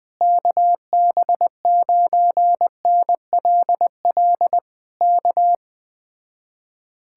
Morse code, heard as a keyed tone, about 20 wpm, 700 Hz. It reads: KB9NLL K
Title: Speed-Racing Common QSO Phrases - 17wpm